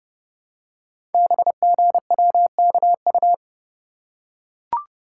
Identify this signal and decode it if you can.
Morse code, heard as a keyed tone, about 28 wpm, 700 Hz. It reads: BGWKU
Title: Code Groups - 5 Letters - 30wpm